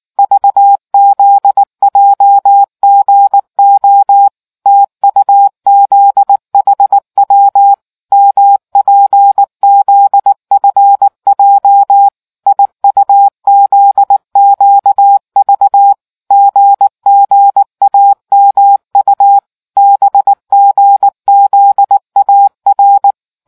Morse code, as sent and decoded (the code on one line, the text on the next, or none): ...- --.. .--- --. --- / - ..- --.. .... .-- / -- .--. --.. ..-. .--- / .. ..- --.. --.- ...- / --. --. .- -- ..- / -... --. --.. .- .-.
VZJGO TUZHW MPZFJ IUZQV GGAMU BGZAR